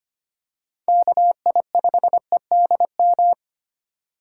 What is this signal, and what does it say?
Morse code, heard as a keyed tone, about 25 wpm, 700 Hz. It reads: KI5EDM
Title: USA Call Signs - 6 Characters - 4x Character Spacing - 25wpm